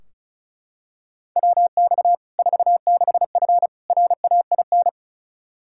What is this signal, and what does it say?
Morse code, heard as a keyed tone, about 36 wpm, 700 Hz. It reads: WX 46F RAIN
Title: Practice QSO - Exchange 2 - Info - 35wpm